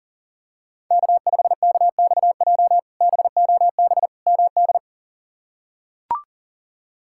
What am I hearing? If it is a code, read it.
Morse code, heard as a keyed tone, about 40 wpm, 700 Hz. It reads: K5KXJ BOB MD
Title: K1USN SST QSO Practice - 40wpm